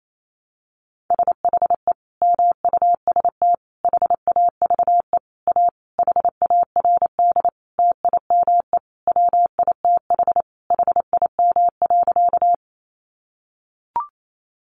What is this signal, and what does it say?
Morse code, heard as a keyed tone, about 28 wpm, 700 Hz. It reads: SHE MUST HAVE A HARD TIME WITH HIM.
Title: Sentences from Top 300 Words 28wpm